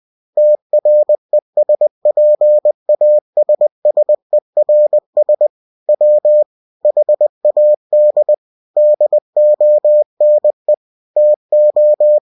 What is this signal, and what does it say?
Morse code, heard as a keyed tone, about 20 wpm, 600 Hz. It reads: TRESPASSERS W HAD DONE TO
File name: cw-books / winnie-the-pooh / winnie-20wpm-page-0022